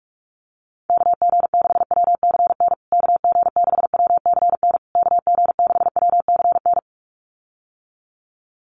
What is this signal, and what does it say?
Morse code, heard as a keyed tone, about 44 wpm, 690 Hz. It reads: KG6WCN KG6WCN KG6WCN
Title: USA Call Signs Repeated 3x 45wpm